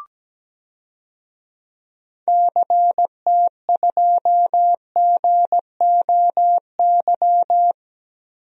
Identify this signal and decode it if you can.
Morse code, heard as a keyed tone, about 17 wpm, 700 Hz. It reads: CT2GOY
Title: DXCC-100 Call Signs - 6 Characters - Repeated 2x - 17wpm